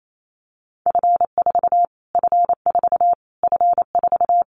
Morse code, heard as a keyed tone, about 28 wpm, 700 Hz. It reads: F4 F4 F4
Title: Callsign Prefixes for Top 25 DXCC - Repeated 3x - 28wpm